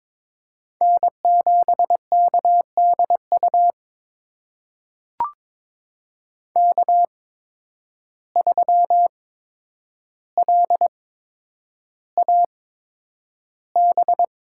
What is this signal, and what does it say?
Morse code, heard as a keyed tone, about 22 wpm, spaced as for 9 wpm, 700 Hz. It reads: N7KDU K3LAB
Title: USA Call Signs - 5 Characters - 8x Character Spacing - 22wpm